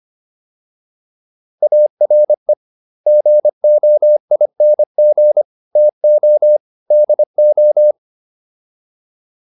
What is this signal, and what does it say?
Morse code, heard as a keyed tone, about 25 wpm, 600 Hz. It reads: ARE GOING TO DO